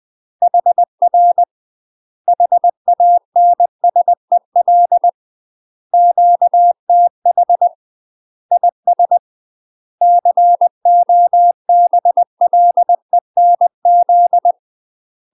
Morse code, heard as a keyed tone, about 20 wpm, 700 Hz. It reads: HR HANSEL QTH IS COBLENZ
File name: UK MASTER QSOs TRAINING / 20WPM / UK MASTER QSOs 0053